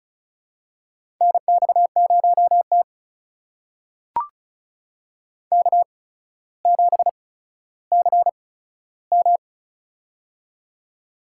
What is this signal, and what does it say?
Morse code, heard as a keyed tone, about 36 wpm, spaced as for 15 wpm, 700 Hz. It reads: NX0T K7CM
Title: USA Call Signs - 4 Characters - 8x Character Spacing - 35wpm